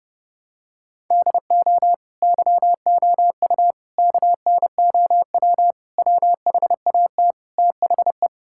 Morse code, heard as a keyed tone, about 30 wpm, 700 Hz. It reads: DO YOU KNOW WHAT THE
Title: Sentences from Top 1600 Words 30wpm